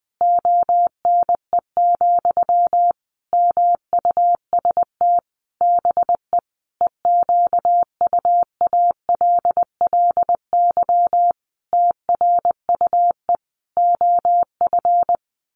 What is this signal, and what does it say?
Morse code, heard as a keyed tone, about 20 wpm, 700 Hz. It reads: ONE, MUST BE EQUALLY TRUE OF
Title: Sentences from Top 1400 Words 20wpm